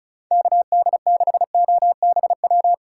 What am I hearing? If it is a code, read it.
Morse code, heard as a keyed tone, about 37 wpm, 700 Hz. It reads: KD6OBW
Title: Speed Racing - USA 2x3 Call Signs 30wpm